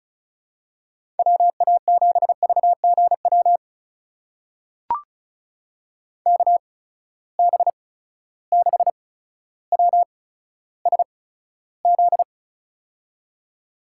Morse code, heard as a keyed tone, about 34 wpm, spaced as for 15 wpm, 700 Hz. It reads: WA7VGW KB6WSZ
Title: USA Call Signs - 8x Character Spacing - 35wpm